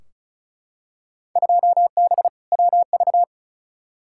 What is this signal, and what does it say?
Morse code, heard as a keyed tone, about 36 wpm, 700 Hz. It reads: 2B WV
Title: Field Day Exchanges Encore - 35wpm